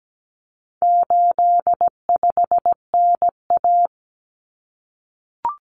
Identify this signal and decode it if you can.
Morse code, heard as a keyed tone, about 17 wpm, 700 Hz. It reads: 85NA
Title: Common Cut Numbers - 4-Digit 17wpm